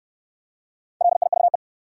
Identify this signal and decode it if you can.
Morse code, heard as a keyed tone, about 67 wpm, 700 Hz. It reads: LIVE